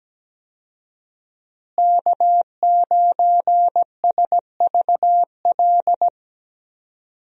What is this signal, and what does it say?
Morse code, heard as a keyed tone, about 17 wpm, 700 Hz. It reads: K9SVL
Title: USA Call Signs - 5 Characters - Repeated 3x - 17wpm